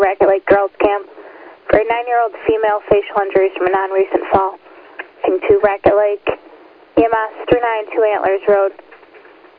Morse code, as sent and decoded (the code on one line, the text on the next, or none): none